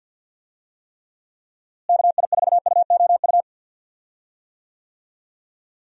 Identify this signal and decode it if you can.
Morse code, heard as a keyed tone, about 47 wpm, 700 Hz. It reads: KI4UOU
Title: USA Call Signs - 6 Characters - Repeated 3x - 50wpm